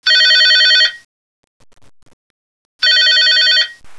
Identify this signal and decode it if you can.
There is no code.